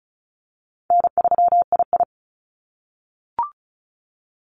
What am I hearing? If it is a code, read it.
Morse code, heard as a keyed tone, about 36 wpm, 700 Hz. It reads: N3II
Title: NA SOTA Activator Callsigns 35wpm